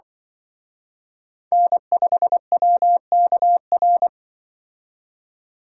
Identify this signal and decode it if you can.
Morse code, heard as a keyed tone, about 25 wpm, 700 Hz. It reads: N5WKR
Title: Speed Racing - USA 1x3 Call Signs 20wpm